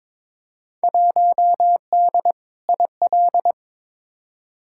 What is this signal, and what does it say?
Morse code, heard as a keyed tone, about 22 wpm, 700 Hz. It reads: 1D IL
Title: Field Day Exchanges - 22wpm